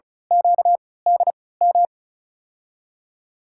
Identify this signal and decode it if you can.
Morse code, heard as a keyed tone, about 34 wpm, spaced as for 20 wpm, 700 Hz. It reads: QDM
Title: USA Call Signs - 6 Characters - 3x Character Spacing - 35wpm